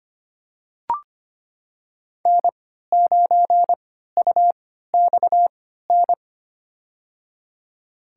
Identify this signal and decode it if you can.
Morse code, heard as a keyed tone, about 25 wpm, spaced as for 14 wpm, 700 Hz. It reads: N9UXN
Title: USA Call Signs - 3x Character Spacing - 25wpm